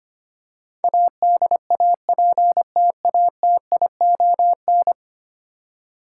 Morse code, heard as a keyed tone, about 25 wpm, 700 Hz. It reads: ADAPTATION
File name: 10 Letter Words - First 3 Letters Given - 25wpm